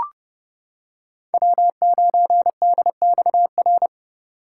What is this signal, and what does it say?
Morse code, heard as a keyed tone, about 30 wpm, 700 Hz. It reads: W9DXR
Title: K1USN SST QSO Practice - 30wpm